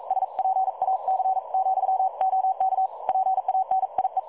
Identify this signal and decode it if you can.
Morse code, heard as a keyed tone, about 42 wpm, 760 Hz. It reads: SPAC, OK PANIE
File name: dlazaawansowanych / nagraniazpasma / websdr recording 3565.0kHz